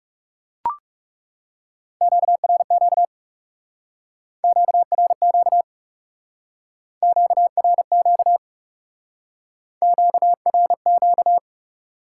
Morse code, heard as a keyed tone, about 39 wpm, 700 Hz. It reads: QRQ QRQ QRQ QRQ